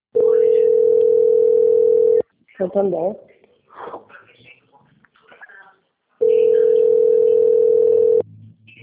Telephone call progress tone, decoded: ringback tone

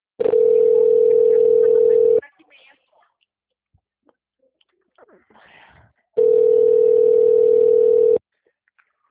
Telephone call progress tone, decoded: ringback tone

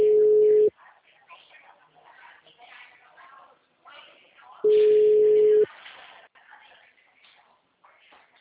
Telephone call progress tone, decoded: ringback tone

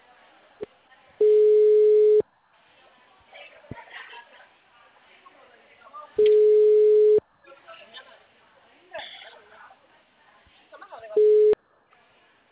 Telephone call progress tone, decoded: ringback tone

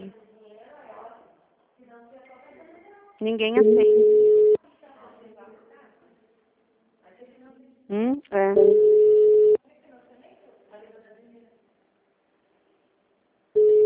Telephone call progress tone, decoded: ringback tone

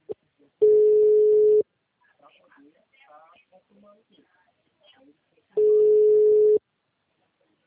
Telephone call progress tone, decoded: ringback tone

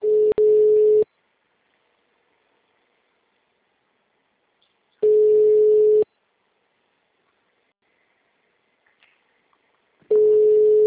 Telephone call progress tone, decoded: ringback tone

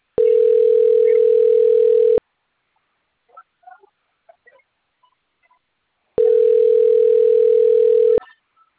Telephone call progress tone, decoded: ringback tone

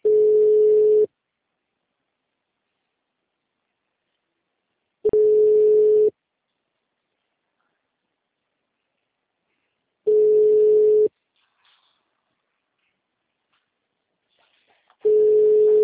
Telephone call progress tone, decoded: ringback tone

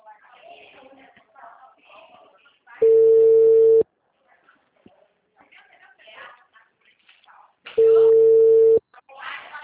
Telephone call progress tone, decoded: ringback tone

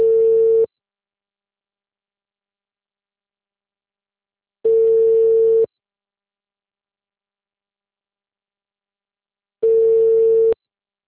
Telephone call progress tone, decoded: ringback tone

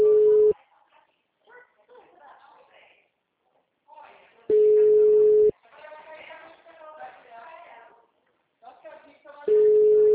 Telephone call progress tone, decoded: ringback tone

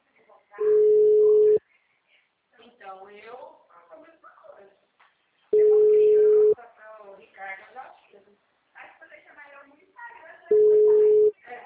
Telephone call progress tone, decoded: ringback tone

